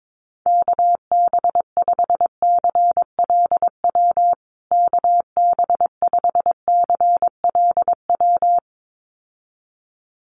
Morse code, heard as a keyed tone, about 22 wpm, 700 Hz. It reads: KB5CLW KB5CLW
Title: USA Call Signs Repeated 2x 22wpm